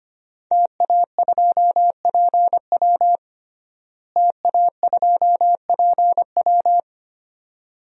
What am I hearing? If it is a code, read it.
Morse code, heard as a keyed tone, about 25 wpm, 700 Hz. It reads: TA2PW TA2PW
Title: DXCC-100 Call Signs - 5 Characters - Repeated 2x - 25wpm